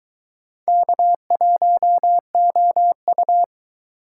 Morse code, heard as a keyed tone, about 22 wpm, 700 Hz. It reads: K1OU